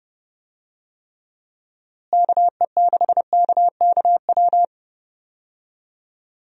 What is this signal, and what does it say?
Morse code, heard as a keyed tone, about 30 wpm, 700 Hz. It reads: KE6KKW
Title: USA Call Signs - 6 Characters - Repeated 3x - 30wpm